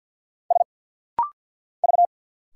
Morse code, heard as a keyed tone, about 50 wpm, 700 Hz. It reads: S V
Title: Single Letters - Rapid-Fire 50wpm